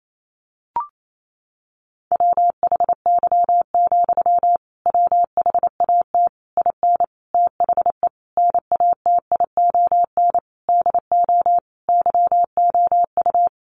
Morse code, heard as a keyed tone, about 28 wpm, 700 Hz. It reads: WHY, WHAT IN THE NATION DO YOU
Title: Sentences from Top 900 Words 28wpm